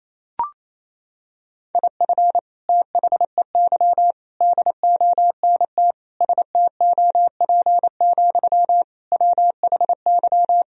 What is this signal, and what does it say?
Morse code, heard as a keyed tone, about 28 wpm, 700 Hz. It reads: IF THEY DONT STOP, WHY
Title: Sentences from Top 1000 Words - Review Part 1 - 28wpm